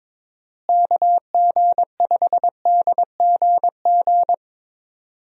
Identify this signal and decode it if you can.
Morse code, heard as a keyed tone, about 21 wpm, 700 Hz. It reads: KG5DGG
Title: USA Call Signs - 6 Characters - Repeated 3x - 22wpm